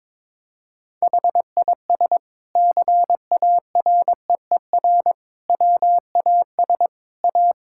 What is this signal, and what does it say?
Morse code, heard as a keyed tone, about 21 wpm, 700 Hz. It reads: HIS CAREER WAS A
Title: Sentences from Top 1000 Words - Review Part 6 - 22wpm